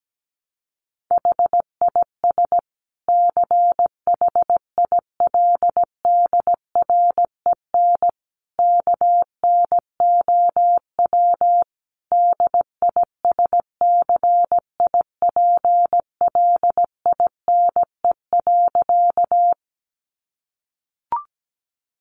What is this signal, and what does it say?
Morse code, heard as a keyed tone, about 17 wpm, 700 Hz. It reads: HIS CHILDREN KNOW DISCIPLINE.
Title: Sentences from Top 900 Words 17wpm